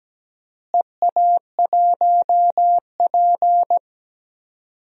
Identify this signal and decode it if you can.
Morse code, heard as a keyed tone, about 17 wpm, 700 Hz. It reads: EA1P
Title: DXCC-100 Call Signs - 4 Characters - Repeated 3x - 17wpm